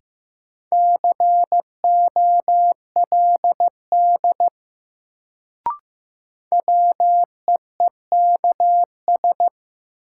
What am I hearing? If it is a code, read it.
Morse code, heard as a keyed tone, about 15 wpm, 700 Hz. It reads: COLD WEEKS
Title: Sets of 5 Words - New Word Drill - 15wpm